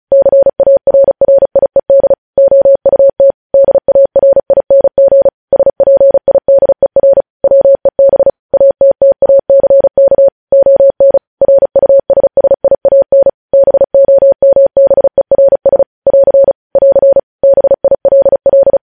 Morse code, heard as a keyed tone, about 36 wpm, 550 Hz. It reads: CARRIED OUT DARING SPIDER WEB ATTACK ON RUSSIAN BOMBERS <AR> <AR> BILL
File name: podcast / News headlines-1x-35x35